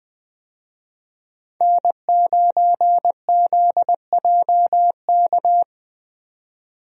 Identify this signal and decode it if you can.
Morse code, heard as a keyed tone, about 20 wpm, 700 Hz. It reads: N9ZJK
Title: USA Call Signs - 5 Characters - 20wpm